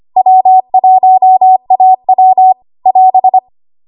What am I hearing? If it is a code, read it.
Morse code, heard as a keyed tone, about 25 wpm, 750 Hz. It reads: W1AW <AS>
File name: Archive / 25 WPM / 25WPM